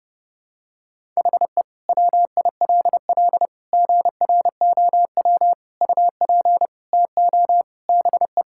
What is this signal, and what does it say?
Morse code, heard as a keyed tone, about 30 wpm, 700 Hz. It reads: HE WILL GROW UP TO BE
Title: Sentences from Top 1000 Words - Review Part 6 - 30wpm